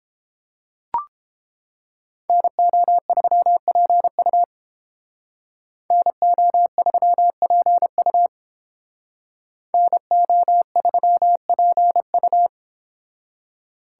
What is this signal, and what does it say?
Morse code, heard as a keyed tone, about 30 wpm, 700 Hz. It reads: NO3PU NO3PU NO3PU